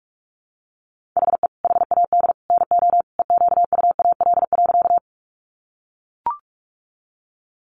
Morse code, heard as a keyed tone, about 44 wpm, 700 Hz. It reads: HE HAD NO EQUAL.
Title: Sentences from Top 1400 Words 45wpm